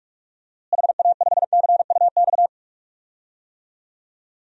Morse code, heard as a keyed tone, about 44 wpm, 700 Hz. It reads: HA5CUX
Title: DXCC-100 Call Signs - 6 Characters - Repeated 2x - 45wpm